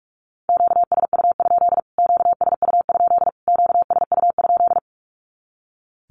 Morse code, heard as a keyed tone, about 44 wpm, 700 Hz. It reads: QSU? QSU? QSU?
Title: Q-Codes Repeated 3x 45wpm